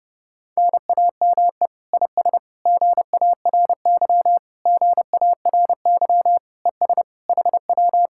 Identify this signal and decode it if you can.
Morse code, heard as a keyed tone, about 30 wpm, 700 Hz. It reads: NAME IS GARY GARY ES HW